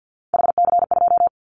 Morse code, heard as a keyed tone, about 50 wpm, 700 Hz. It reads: HC2
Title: Callsign Prefixes for Top 100 DXCC - 50wpm